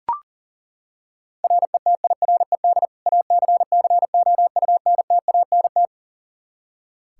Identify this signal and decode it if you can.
Morse code, heard as a keyed tone, about 40 wpm, 700 Hz. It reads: RETIRED ACCOUNTANT
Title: Practice QSO - Exchange 3 - More Info - 40wpm